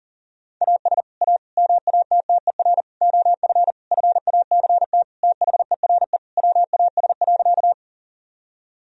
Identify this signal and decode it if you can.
Morse code, heard as a keyed tone, about 40 wpm, 700 Hz. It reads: AS A MATTER OF FACT THERE WAS.
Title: Sentences from Top 1000 Words - Review Part 3 - 40wpm